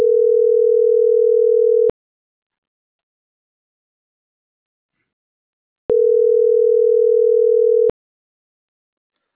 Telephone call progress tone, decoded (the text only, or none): ringback tone